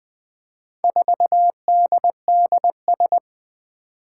Morse code, heard as a keyed tone, about 20 wpm, 700 Hz. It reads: 4DDS
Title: Four Challenging Letter-Number Combinations 20wpm